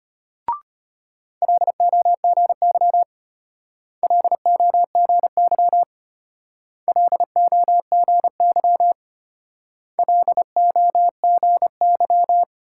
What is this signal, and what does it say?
Morse code, heard as a keyed tone, about 33 wpm, 700 Hz. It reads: LOGY LOGY LOGY LOGY